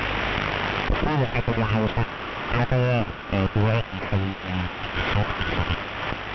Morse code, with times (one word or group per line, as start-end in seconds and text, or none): none